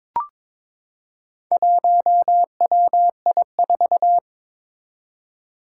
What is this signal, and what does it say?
Morse code, heard as a keyed tone, about 22 wpm, 700 Hz. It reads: 1WI4
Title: Four Challenging Letter-Number Combinations 22wpm